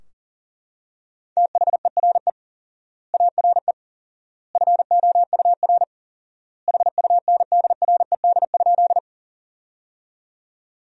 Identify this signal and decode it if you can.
Morse code, heard as a keyed tone, about 37 wpm, 700 Hz. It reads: THERE ARE FOUR HUNDRED?
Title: 4x Word-Spacing - Sentences from Top 400 Words - 40wpm